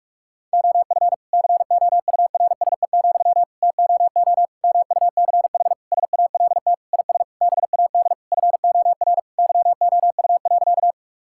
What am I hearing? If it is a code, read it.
Morse code, heard as a keyed tone, about 46 wpm, 700 Hz. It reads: OF COURSE, TOO MUCH SALT IS BAD FOR YOU.
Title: Sentences from Top 1000 Words - Review Part 5 - 45wpm